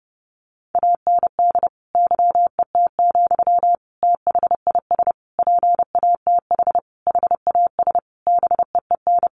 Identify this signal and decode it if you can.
Morse code, heard as a keyed tone, about 30 wpm, 700 Hz. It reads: AND YET, THIS PATH HAS BEEN
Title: Sentences from Top 1400 Words 30wpm